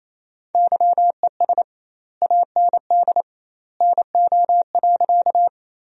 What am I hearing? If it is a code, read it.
Morse code, heard as a keyed tone, about 28 wpm, 700 Hz. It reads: YES AND NO.
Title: 2x Word-Spacing - Sentences from Top 500 Words - 28wpm